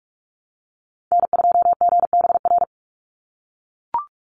Morse code, heard as a keyed tone, about 44 wpm, 700 Hz. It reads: N2GBR
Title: NA SOTA Chaser Callsigns 45wpm